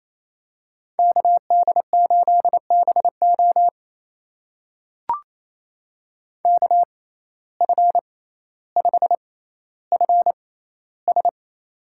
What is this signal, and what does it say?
Morse code, heard as a keyed tone, about 28 wpm, spaced as for 14 wpm, 700 Hz. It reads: KD8BO KF5FS
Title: USA Call Signs - 5 Characters - 6x Character Spacing - 28wpm